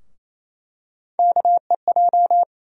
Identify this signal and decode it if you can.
Morse code, heard as a keyed tone, about 28 wpm, 700 Hz. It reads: KEJ
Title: USA Call Sign Postfixes - 3 Letters 28wpm